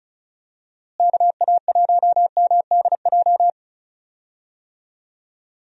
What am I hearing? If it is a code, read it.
Morse code, heard as a keyed tone, about 36 wpm, 700 Hz. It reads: KA1MDJ
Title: USA Call Signs - 6 Characters - Repeated 3x - 35wpm